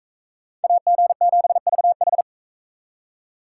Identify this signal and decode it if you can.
Morse code, heard as a keyed tone, about 44 wpm, 700 Hz. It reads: AG7VH